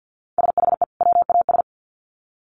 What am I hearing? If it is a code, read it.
Morse code, heard as a keyed tone, about 50 wpm, 700 Hz. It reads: SHE WAS
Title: Sets of 2 Words 50wpm